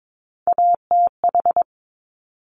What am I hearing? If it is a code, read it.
Morse code, heard as a keyed tone, about 22 wpm, 700 Hz. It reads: ATH
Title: Top Three Letter Combinations in Words 22wpm